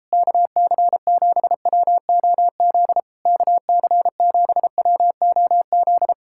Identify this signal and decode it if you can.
Morse code, heard as a keyed tone, about 31 wpm, 700 Hz. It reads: KC7WOZ KC7WOZ
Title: Speed-Racing Common QSO Phrases - 28wpm